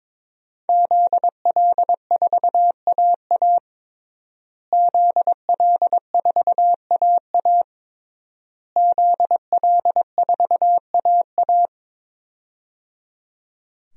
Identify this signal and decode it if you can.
Morse code, heard as a keyed tone, about 23 wpm, 700 Hz. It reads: ZL4AA ZL4AA ZL4AA
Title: DXCC-100 Call Signs - 5 Characters - Repeated 3x - 22wpm